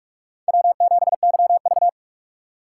Morse code, heard as a keyed tone, about 46 wpm, 690 Hz. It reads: W7YV
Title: USA Call Signs - 4 Characters - 8x Character Spacing - 45wpm